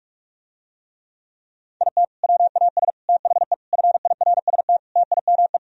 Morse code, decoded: IT WAS THE FIRST TIME

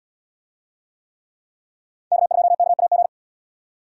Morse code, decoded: K2KAG

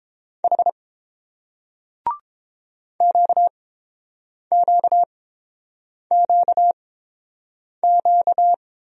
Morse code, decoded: H QQQQ

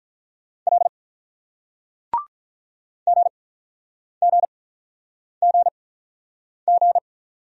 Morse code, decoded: L GGGG

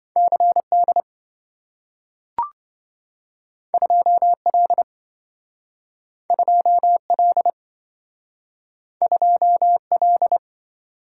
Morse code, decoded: CD 2L 2L 2L

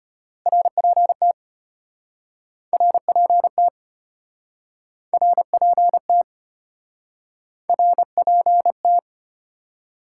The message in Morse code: RPT RPT RPT RPT

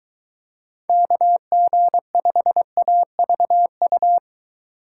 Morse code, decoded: KG5AVU

